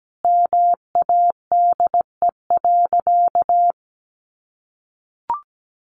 Morse code, heard as a keyed tone, about 17 wpm, 700 Hz. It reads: MADE.
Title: Sentences from Top 1500 Words 17wpm